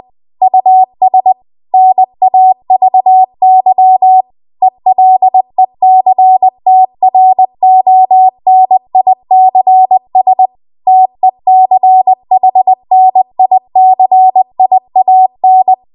Morse code, heard as a keyed tone, about 20 wpm, 750 Hz. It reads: US NAVY ELECTRONICS TECHNICIAN
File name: Archive / 20 WPM / 20WPM